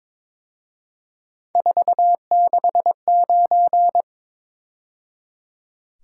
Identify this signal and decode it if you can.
Morse code, heard as a keyed tone, about 26 wpm, 700 Hz. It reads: 469
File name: Speed-Racing - Three Letter-Number Combinations - 22wpm